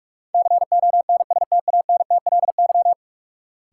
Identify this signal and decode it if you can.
Morse code, heard as a keyed tone, about 46 wpm, 700 Hz. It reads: CONSTANTLY
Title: Sentences from Top 1000 Words - Review Part 7 - New Word Drill - 45wpm